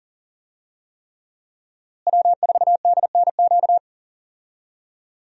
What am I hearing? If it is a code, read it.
Morse code, heard as a keyed tone, about 39 wpm, 700 Hz. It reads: W4DNQ
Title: USA Call Signs - 5 Characters - Repeated 3x - 40wpm